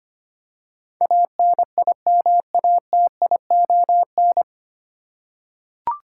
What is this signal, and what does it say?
Morse code, heard as a keyed tone, about 25 wpm, 700 Hz. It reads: ANIMATION
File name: practice / 25wpm / 9 Letter Words - First 3 Letters Given - 25wpm